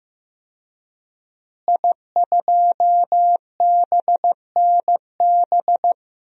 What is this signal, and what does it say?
Morse code, heard as a keyed tone, about 15 wpm, 700 Hz. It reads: I2BNB
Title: DXCC-100 Call Signs - 5 Characters - Repeated 2x - 15wpm